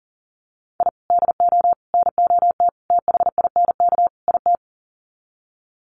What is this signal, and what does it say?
Morse code, heard as a keyed tone, about 37 wpm, 700 Hz. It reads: I DO NOT THINK IT
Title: Sets of 5 Words 40wpm